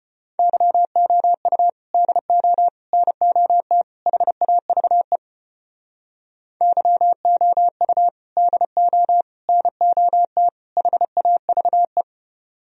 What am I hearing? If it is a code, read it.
Morse code, heard as a keyed tone, about 33 wpm, 700 Hz. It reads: YOU DO NOT HAVE YOU DO NOT HAVE